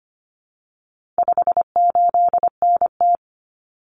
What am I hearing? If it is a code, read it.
Morse code, heard as a keyed tone, about 25 wpm, 700 Hz. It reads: 58NT
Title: Common Cut Numbers - 4-Digit 25wpm